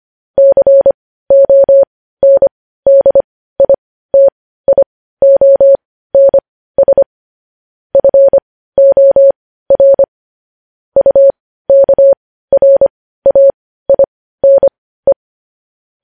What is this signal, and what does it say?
Morse code, heard as a keyed tone, about 25 wpm, spaced as for 15 wpm, 550 Hz. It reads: CONDITIONS FOR UKRAINE
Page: news headlines in morse code at 15 wpm